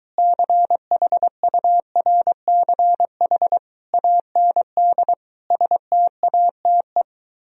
Morse code, heard as a keyed tone, about 24 wpm, 700 Hz. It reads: CHURCH AND STATE